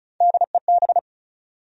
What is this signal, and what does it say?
Morse code, heard as a keyed tone, about 36 wpm, 700 Hz. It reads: DEB
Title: USA Common Names - Max 5 Letters - 35wpm